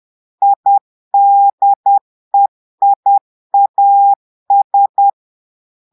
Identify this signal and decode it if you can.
Morse code, heard as a keyed tone, about 10 wpm, 800 Hz. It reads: IDEIAS
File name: portugais texte en clair 2 vitesse 10 mots minutes